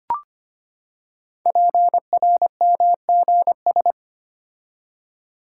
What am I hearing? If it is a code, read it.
Morse code, heard as a keyed tone, about 25 wpm, 700 Hz. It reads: PRMGS
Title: Code Groups - 5 Letters - 25wpm